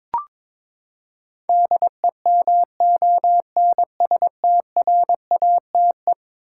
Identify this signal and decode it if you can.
Morse code, heard as a keyed tone, about 22 wpm, 700 Hz. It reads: DEMONSTRATE
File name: practice / 22wpm / Eleven Letter Words - 22wpm